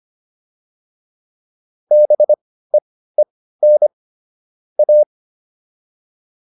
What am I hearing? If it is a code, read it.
Morse code, heard as a keyed tone, about 25 wpm, spaced as for 15 wpm, 600 Hz. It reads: BEEN A